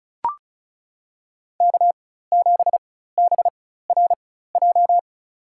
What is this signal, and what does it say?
Morse code, heard as a keyed tone, about 34 wpm, spaced as for 16 wpm, 700 Hz. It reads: K7BRJ
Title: USA Call Signs - 5 Characters - 4x Character Spacing - 35wpm